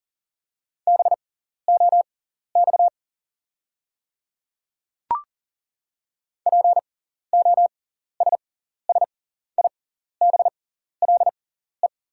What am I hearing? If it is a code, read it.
Morse code, heard as a keyed tone, about 40 wpm, spaced as for 14 wpm, 690 Hz. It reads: BOX POSSIBLE
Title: 6x Character Spacing - Top 500 Words - 40wpm